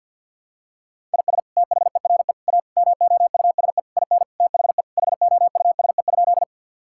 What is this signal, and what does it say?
Morse code, heard as a keyed tone, about 51 wpm, 700 Hz. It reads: IS THERE A MOUSE IN THE HOUSE?
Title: Sentences from Top 1000 Words - Review Part 5 - 50wpm